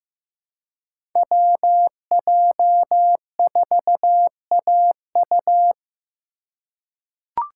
Morse code, heard as a keyed tone, about 15 wpm, 700 Hz. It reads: WJ4AU
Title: Code Groups - 5 Letter:Numbers - 15wpm